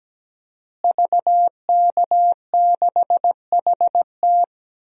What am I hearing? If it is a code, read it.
Morse code, heard as a keyed tone, about 17 wpm, 700 Hz. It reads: VK6HT
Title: DXCC-100 Call Signs - 5 Characters - Repeated 3x - 17wpm